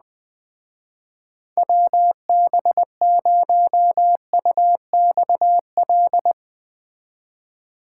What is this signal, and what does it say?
Morse code, heard as a keyed tone, about 21 wpm, 700 Hz. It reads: WB0UXL